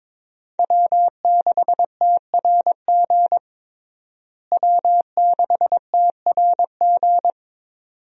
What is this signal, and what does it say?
Morse code, heard as a keyed tone, about 23 wpm, 700 Hz. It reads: W6TRG W6TRG
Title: USA Call Signs - 5 Characters - Repeated 2x - 22wpm